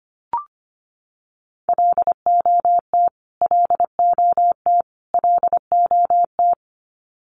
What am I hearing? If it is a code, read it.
Morse code, heard as a keyed tone, about 25 wpm, 700 Hz. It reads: LOT LOT LOT
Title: Top 500 Words Repeated 25wpm